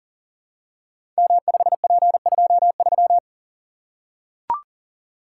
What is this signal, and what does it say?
Morse code, heard as a keyed tone, about 39 wpm, 690 Hz. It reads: M5P23
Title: Code Groups - 5 Letter:Numbers - 40wpm